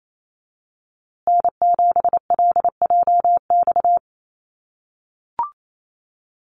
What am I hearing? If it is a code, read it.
Morse code, heard as a keyed tone, about 28 wpm, 700 Hz. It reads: N7LJX
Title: USA Call Signs Repeated 2x 28wpm